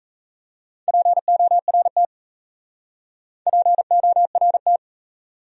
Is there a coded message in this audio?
Yes: Morse code from 0.9 to 4.8 s, heard as a keyed tone, about 36 wpm, 700 Hz. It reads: PORT PORT